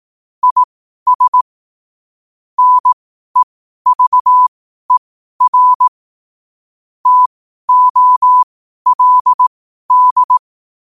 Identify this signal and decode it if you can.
Morse code, heard as a keyed tone, about 18 wpm, spaced as for 12 wpm, 1000 Hz. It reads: IS NEVER TOLD
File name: qotdfiles / 12wpm